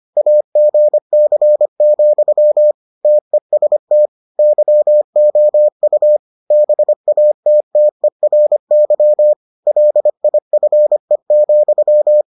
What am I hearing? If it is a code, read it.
Morse code, heard as a keyed tone, about 25 wpm, 600 Hz. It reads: AGC, TEST YOU BATTERY LIFE,